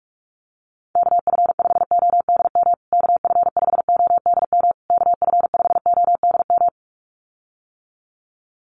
Morse code, heard as a keyed tone, about 44 wpm, 700 Hz. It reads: KF5ODM KF5ODM KF5ODM
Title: USA Call Signs Repeated 3x 45wpm